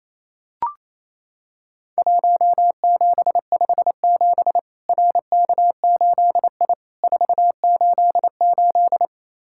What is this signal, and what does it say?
Morse code, heard as a keyed tone, about 28 wpm, 700 Hz. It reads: 1757 RK8I 488
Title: Worked All Europe DX Contest - QTC Exchange - 28wpm